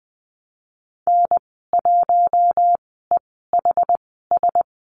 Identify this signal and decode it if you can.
Morse code, heard as a keyed tone, about 20 wpm, spaced as for 14 wpm, 700 Hz. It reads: N1EHS
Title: 069 Lesson 12g - Letter d - Call Signs 20wpm